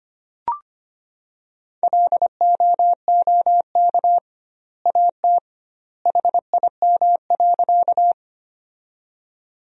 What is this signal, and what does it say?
Morse code, heard as a keyed tone, about 25 wpm, 700 Hz. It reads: LOOK AT HIM.
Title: 2x Word-Spacing - Sentences from Top 100 Words - 25wpm